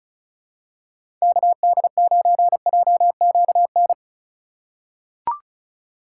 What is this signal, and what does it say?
Morse code, heard as a keyed tone, about 36 wpm, 700 Hz. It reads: KD9JQN
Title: USA Call Signs - 6 Characters - 2x Character Spacing - 35wpm